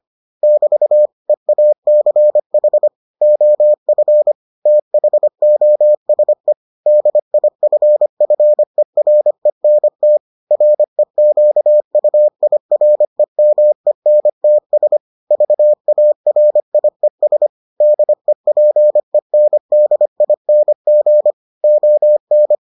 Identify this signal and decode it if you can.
Morse code, heard as a keyed tone, about 25 wpm, 600 Hz. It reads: <BT> EACH OF THOSE DIFFERENT REQUIREMENTS VARIES DEPENDING ON